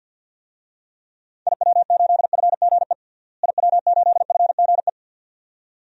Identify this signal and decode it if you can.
Morse code, heard as a keyed tone, about 51 wpm, 700 Hz. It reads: IW8FGE IW8FGE
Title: DXCC-100 Call Signs - 6 Characters - Repeated 2x - 50wpm